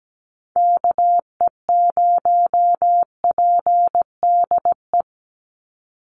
Morse code, heard as a keyed tone, about 17 wpm, 700 Hz. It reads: KE0PDE
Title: USA Call Signs 17wpm